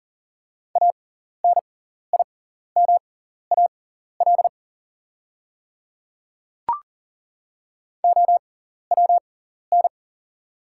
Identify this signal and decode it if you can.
Morse code, heard as a keyed tone, about 40 wpm, spaced as for 14 wpm, 690 Hz. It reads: ANIMAL OWN
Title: 6x Character Spacing - Top 300 Words - 40wpm